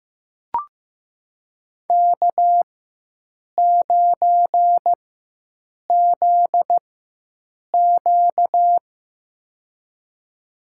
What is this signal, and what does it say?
Morse code, heard as a keyed tone, about 15 wpm, spaced as for 7 wpm, 700 Hz. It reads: K9ZQ
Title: USA Call Signs - 4 Characters - 4x Character Spacing - 15wpm